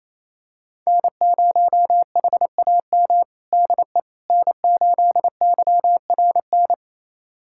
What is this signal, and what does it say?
Morse code, heard as a keyed tone, about 28 wpm, 700 Hz. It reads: N0HAM DE N8YRN
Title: USA Callsign DE Callsign - 5 Characters - 28wpm